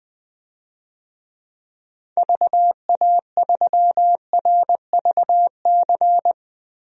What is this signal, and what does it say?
Morse code, heard as a keyed tone, about 20 wpm, 700 Hz. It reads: VA3RVC
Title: DXCC-100 Call Signs - 6 Characters - 20wpm